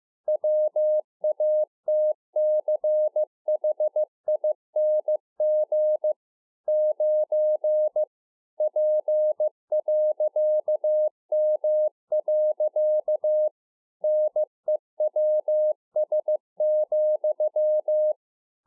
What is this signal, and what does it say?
Morse code, heard as a keyed tone, about 15 wpm, 610 Hz. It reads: WATCHING 9 P.M. NEWS,
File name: 15WPM